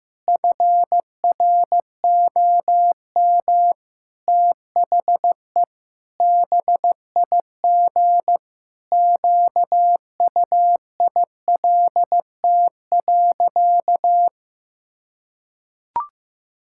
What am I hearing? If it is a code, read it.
Morse code, heard as a keyed tone, about 15 wpm, 700 Hz. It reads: FROM THE BIG QUILT.
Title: Pangram Challenge v2 - 15wpm